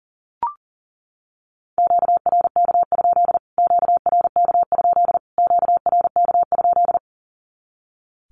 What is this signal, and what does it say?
Morse code, heard as a keyed tone, about 40 wpm, 700 Hz. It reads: QRK? QRK? QRK?
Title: Q-Codes Repeated 3x 40wpm